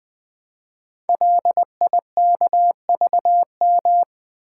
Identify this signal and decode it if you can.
Morse code, heard as a keyed tone, about 20 wpm, 700 Hz. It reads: LIKVM
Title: Code Groups - 5 Letters - 20wpm